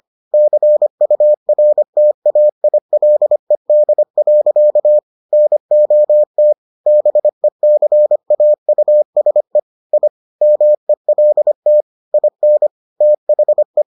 Morse code, heard as a keyed tone, about 25 wpm, 600 Hz. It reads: CURTAILED. NOT BECAUSE I MELT IN THE